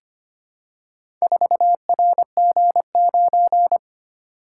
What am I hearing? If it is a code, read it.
Morse code, heard as a keyed tone, about 25 wpm, 700 Hz. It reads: 4RG9
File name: practice / 25wpm / Four Letter-Number Combinations - 25wpm